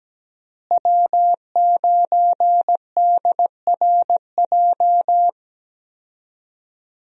Morse code, heard as a keyed tone, about 17 wpm, 700 Hz. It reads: W9DRJ
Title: USA Call Signs - 5 Characters - Repeated 3x - 17wpm